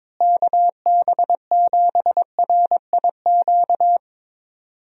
Morse code, heard as a keyed tone, about 22 wpm, 700 Hz. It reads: KB7RIQ
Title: USA Call Signs - 6 Characters - Repeated 3x - 22wpm